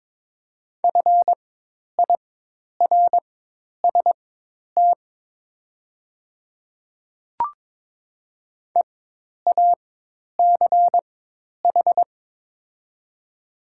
Morse code, heard as a keyed tone, about 22 wpm, spaced as for 10 wpm, 700 Hz. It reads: FIRST EACH
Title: 4x Character Spacing - Top 100 Words - 22wpm